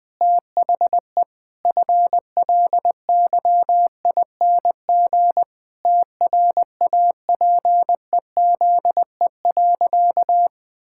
Morse code, heard as a keyed tone, about 20 wpm, 700 Hz. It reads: THE FLYING TRAPEZE.
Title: Pangram Challenge v2 - 20wpm